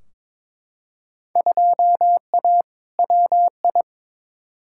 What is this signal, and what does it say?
Morse code, heard as a keyed tone, about 22 wpm, 700 Hz. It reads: 2A WI
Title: Field Day Exchanges - 22wpm